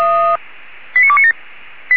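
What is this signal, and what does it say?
Five-tone selective call (ZVEI), heard as quick runs of five tones, tones of about 70 ms. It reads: 89297
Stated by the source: ZVEI five-tone call 89288 received with an SDR by fire brigade Feuerwehr Orlamünde.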